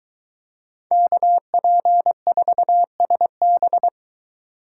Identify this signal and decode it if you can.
Morse code, heard as a keyed tone, about 23 wpm, 700 Hz. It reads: KP4SB